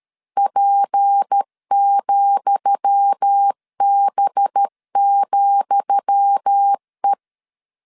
Morse code, heard as a keyed tone, about 13 wpm, 800 Hz. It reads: P,B,E
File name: LEZIONI / 10 SIGNS / 10D COMA 12wpm 2544